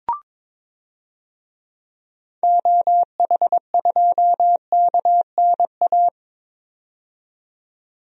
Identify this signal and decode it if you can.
Morse code, heard as a keyed tone, about 22 wpm, 700 Hz. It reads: OH2KNA
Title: DXCC-100 Call Signs - 6 Characters - 22wpm